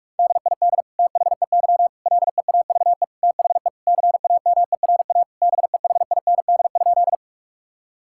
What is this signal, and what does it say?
Morse code, heard as a keyed tone, about 46 wpm, 700 Hz. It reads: DID THEY LEAVE THE CAMERA BEHIND?
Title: Sentences from Top 1000 Words - Review Part 5 - 45wpm